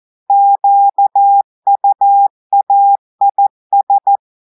Morse code, heard as a keyed tone, about 14 wpm, 800 Hz. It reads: QUAIS
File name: audios / portugais / portugais texte en clair 2 vitesse 14 mots minutes